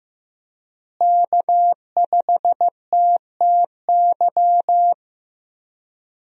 Morse code, heard as a keyed tone, about 15 wpm, 700 Hz. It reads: K5TTY
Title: USA Call Signs - 5 Characters - Repeated 3x - 15wpm